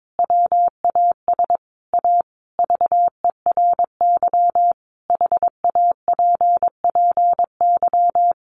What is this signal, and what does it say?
Morse code, heard as a keyed tone, about 22 wpm, 700 Hz. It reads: WAS A VERY HAPPY
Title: Sentences from Top 1400 Words 22wpm